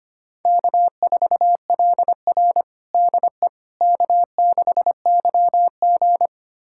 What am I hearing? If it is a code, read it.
Morse code, heard as a keyed tone, about 25 wpm, 700 Hz. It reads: K4LR DE K6YG
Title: USA Callsign DE Callsign - 4 Characters - 25wpm